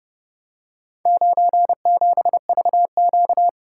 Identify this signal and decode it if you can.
Morse code, heard as a keyed tone, about 29 wpm, 700 Hz. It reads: 97VQ